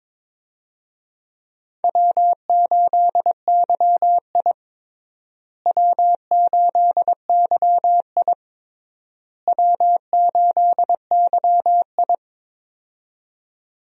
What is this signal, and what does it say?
Morse code, heard as a keyed tone, about 22 wpm, 700 Hz. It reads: W8YI W8YI W8YI
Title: USA Call Signs - 4 Characters - Repeated 3x - 22wpm